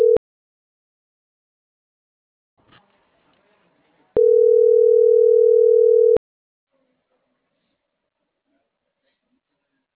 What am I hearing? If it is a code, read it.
Ringback tone (North America), 440 + 480 Hz.